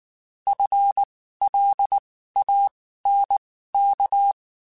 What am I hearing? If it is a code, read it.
Morse code, heard as a keyed tone, about 19 wpm, spaced as for 14 wpm, 800 Hz. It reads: FLANK
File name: Spaced / FL Runs sp mp3s / FL Runs sp @18